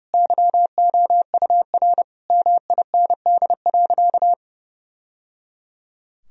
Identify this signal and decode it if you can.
Morse code, heard as a keyed tone, about 30 wpm, 700 Hz. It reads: YOUR MIND.
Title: Sentences from Top 1000 Words - Review Part 3 - 30wpm